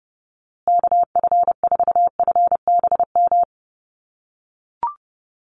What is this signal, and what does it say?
Morse code, heard as a keyed tone, about 28 wpm, 700 Hz. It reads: KF4FBM
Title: USA Call Signs Repeated 2x 30wpm